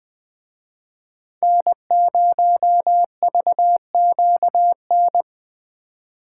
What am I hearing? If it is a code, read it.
Morse code, heard as a keyed tone, about 24 wpm, 700 Hz. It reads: N0VQN
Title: Speed Racing - USA Call Signs 20wpm